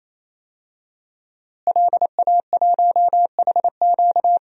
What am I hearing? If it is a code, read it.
Morse code, heard as a keyed tone, about 28 wpm, 700 Hz. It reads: LA1HQ
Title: DXCC-100 Call Signs - 5 Characters - Repeated 3x - 28wpm